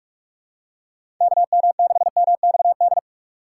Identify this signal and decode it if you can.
Morse code, heard as a keyed tone, about 47 wpm, 700 Hz. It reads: KM6MXD
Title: USA Call Signs - 6 Characters - 4x Character Spacing - 45wpm